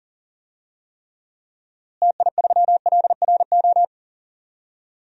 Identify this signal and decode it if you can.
Morse code, heard as a keyed tone, about 40 wpm, 700 Hz. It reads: TI3LRO
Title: DXCC-100 Call Signs - 6 Characters - Repeated 2x - 40wpm